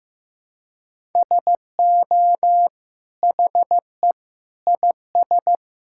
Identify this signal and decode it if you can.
Morse code, heard as a keyed tone, about 16 wpm, 690 Hz. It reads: SO HE IS
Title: Sentences from Top 1000 Words - Review Part 5 - 15wpm